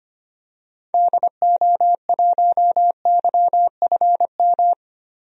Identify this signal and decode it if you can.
Morse code, heard as a keyed tone, about 25 wpm, 700 Hz. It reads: DO1YFM
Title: DXCC-100 Call Signs - 6 Characters - Repeated 2x - 25wpm